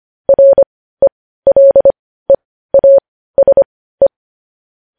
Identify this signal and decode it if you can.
Morse code, heard as a keyed tone, about 25 wpm, spaced as for 15 wpm, 550 Hz. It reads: RELEASE
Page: news-headlines-in-morse-code-at-15-wpm